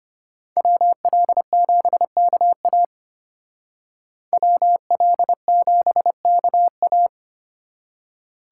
Morse code, heard as a keyed tone, about 32 wpm, 700 Hz. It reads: WL7KA WL7KA